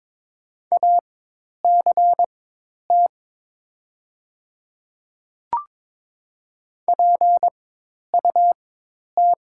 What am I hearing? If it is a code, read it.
Morse code, heard as a keyed tone, about 23 wpm, spaced as for 10 wpm, 700 Hz. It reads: ACT PUT
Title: 4x Character Spacing - Top 500 Words - 22wpm